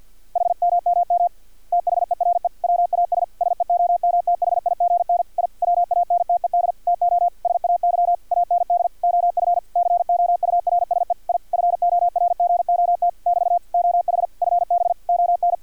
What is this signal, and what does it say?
Morse code, heard as a keyed tone, about 50 wpm, 700 Hz. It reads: HMMM THERE WAS SEOMTHIGN I WANTED TO SAY AND OF COURSE I FORGOT <BT> OH FB ON
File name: archived / CRSnet